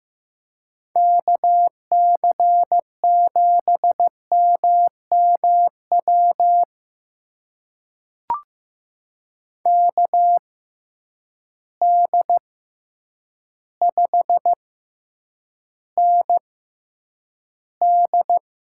Morse code, heard as a keyed tone, about 15 wpm, spaced as for 8 wpm, 700 Hz. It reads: KC7MMW KD5ND
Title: USA Call Signs - 6 Characters - 6x Character Spacing - 15wpm